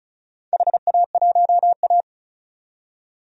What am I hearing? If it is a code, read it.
Morse code, heard as a keyed tone, about 36 wpm, 690 Hz. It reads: HA1A
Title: DXCC-100 Call Signs - 4 Characters - 35wpm